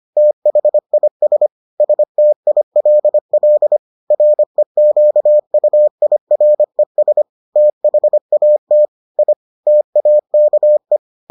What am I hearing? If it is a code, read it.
Morse code, heard as a keyed tone, about 25 wpm, 600 Hz. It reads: THIS STILL REQUIRES THAT I TAKE